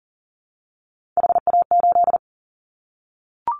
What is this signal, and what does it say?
Morse code, heard as a keyed tone, about 41 wpm, 700 Hz. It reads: HA8
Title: Callsign Prefixes for Top 50 DXCC - 40wpm